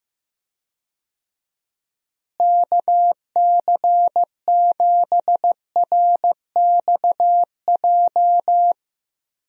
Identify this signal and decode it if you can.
Morse code, heard as a keyed tone, about 16 wpm, 700 Hz. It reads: KC7RXJ
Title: USA Call Signs - 6 Characters - Repeated 3x - 15wpm